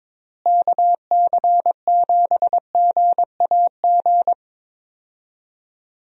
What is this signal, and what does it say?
Morse code, heard as a keyed tone, about 21 wpm, 700 Hz. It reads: KC7GAG
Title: USA Call Signs - 6 Characters - Repeated 2x - 22wpm